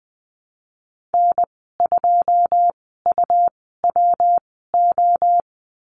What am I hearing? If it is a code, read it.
Morse code, heard as a keyed tone, about 20 wpm, spaced as for 15 wpm, 700 Hz. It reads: N2UWO
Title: 122 Lesson 20g - Number 3 - Call Signs 20wpm